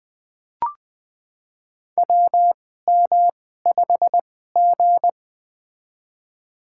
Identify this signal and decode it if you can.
Morse code, heard as a keyed tone, about 20 wpm, spaced as for 15 wpm, 700 Hz. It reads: WM5G
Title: USA Call Signs - 4 Characters - 2x Character Spacing - 20wpm